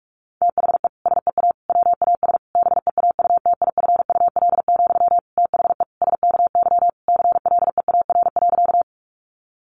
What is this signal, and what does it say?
Morse code, heard as a keyed tone, about 44 wpm, 700 Hz. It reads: THE SEA WAS BEAUTIFUL, THE SKY CLEAR.
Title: Sentences from Top 1200 Words 45wpm